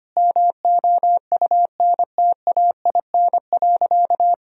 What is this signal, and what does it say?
Morse code, heard as a keyed tone, about 25 wpm, 700 Hz. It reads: MOUNTAIN.
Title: Sentences from Top 1000 Words - Review Part 1 - 25wpm